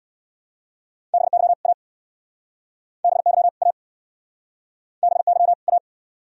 Morse code, heard as a keyed tone, about 66 wpm, 700 Hz. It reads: BY A BY A BY A